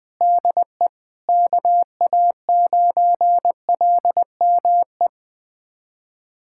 Morse code, heard as a keyed tone, about 20 wpm, 700 Hz. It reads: DE KA9LME
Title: USA Callsign DE Callsign 20wpm